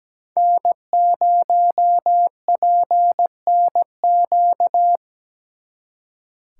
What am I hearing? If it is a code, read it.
Morse code, heard as a keyed tone, about 21 wpm, 700 Hz. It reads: N0PNQ
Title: Speed Racing - USA 1x3 Call Signs 17wpm